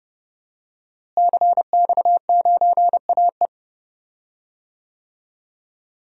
Morse code, heard as a keyed tone, about 30 wpm, 700 Hz. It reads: CX9AE
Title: DXCC-100 Call Signs - 5 Characters - Repeated 2x - 30wpm